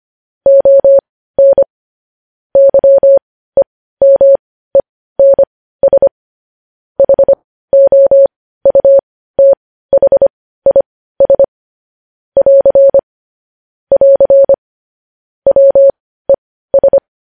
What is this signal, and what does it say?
Morse code, heard as a keyed tone, about 25 wpm, spaced as for 15 wpm, 550 Hz. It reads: ON YEMENS HOUTHIS <AR> <AR> WES
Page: news-headlines-in-morse-code-at-15-wpm